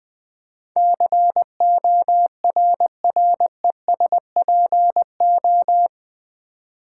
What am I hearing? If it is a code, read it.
Morse code, heard as a keyed tone, about 20 wpm, 700 Hz. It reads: CORRESPO
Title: Build A Word - 4 Letters At A Time - 10-Letter Words - 20wpm